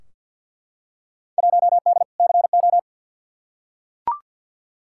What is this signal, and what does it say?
Morse code, heard as a keyed tone, about 52 wpm, 700 Hz. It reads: 1D CO
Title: Field Day Exchanges - 50wpm